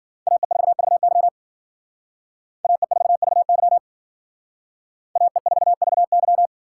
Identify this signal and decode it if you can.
Morse code, heard as a keyed tone, about 48 wpm, 700 Hz. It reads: AE4VY AE4VY AE4VY
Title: Speed Racing - USA 2x2 Call Signs 40wpm